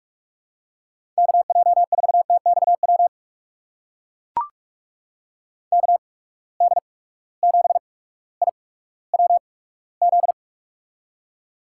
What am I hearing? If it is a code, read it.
Morse code, heard as a keyed tone, about 45 wpm, spaced as for 19 wpm, 700 Hz. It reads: KJ4TXW KD7IWZ
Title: USA Call Signs - 6 Characters - 8x Character Spacing - 45wpm